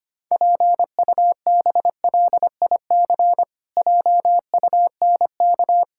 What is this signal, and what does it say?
Morse code, heard as a keyed tone, about 25 wpm, 700 Hz. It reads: PUBLIC JUNK
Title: Pangram Challenge v2 - 25wpm